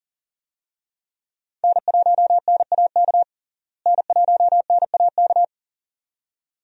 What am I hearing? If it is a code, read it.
Morse code, heard as a keyed tone, about 40 wpm, 700 Hz. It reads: N1NAK N1NAK
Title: USA Call Signs - 5 Characters - Repeated 2x - 40wpm